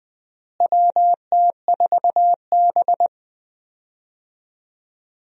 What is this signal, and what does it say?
Morse code, heard as a keyed tone, about 20 wpm, 700 Hz. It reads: WT4B
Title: USA Call Signs - 4 Characters - Repeated 3x - 20wpm